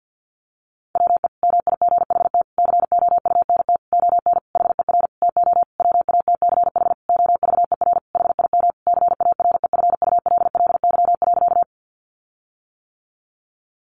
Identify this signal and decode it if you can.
Morse code, heard as a keyed tone, about 50 wpm, 700 Hz. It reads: WE MIGHT COUNT ON HER TO WATCH OVER HIM CAREFULLY.
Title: Sentences from Top 1500 Words 50wpm